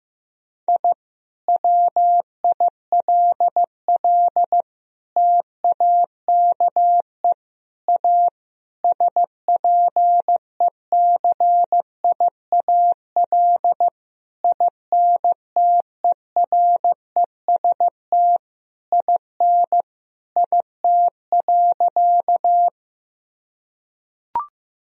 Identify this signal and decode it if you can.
Morse code, heard as a keyed tone, about 16 wpm, 700 Hz. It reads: I WILL TAKE A SPECIAL INTEREST IN IT.
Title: Sentences from Top 1000 Words - Review Part 3 - 15wpm